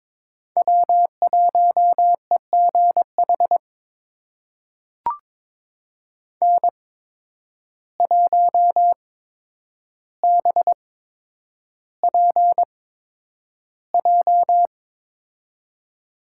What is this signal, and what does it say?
Morse code, heard as a keyed tone, about 23 wpm, spaced as for 10 wpm, 700 Hz. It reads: W1EGH N1BPJ
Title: USA Call Signs - 5 Characters - 8x Character Spacing - 22wpm